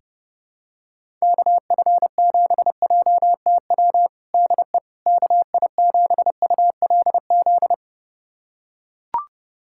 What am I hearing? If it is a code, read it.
Morse code, heard as a keyed tone, about 32 wpm, 700 Hz. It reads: KF7JTW DE KI7ULZ
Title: USA Callsign DE Callsign 30wpm